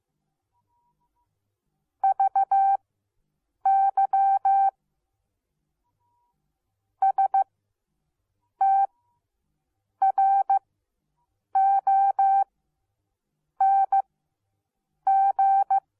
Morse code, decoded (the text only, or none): VY STRONG